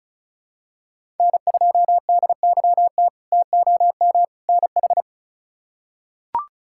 Morse code, heard as a keyed tone, about 34 wpm, 700 Hz. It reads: N2DYT TOM NH
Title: K1USN SST QSO Practice - 35wpm